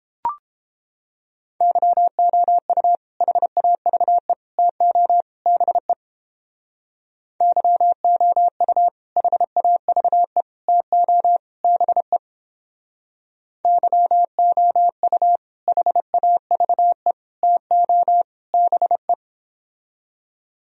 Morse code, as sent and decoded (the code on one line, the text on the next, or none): -.-- --- ..- / .... .- ...- . / - --- / -... / -.-- --- ..- / .... .- ...- . / - --- / -... / -.-- --- ..- / .... .- ...- . / - --- / -... .
YOU HAVE TO BE YOU HAVE TO BE YOU HAVE TO BE